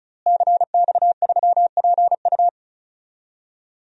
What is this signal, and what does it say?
Morse code, heard as a keyed tone, about 34 wpm, 700 Hz. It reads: CX3PU
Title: DXCC-100 Call Signs - 5 Characters - Repeated 2x - 35wpm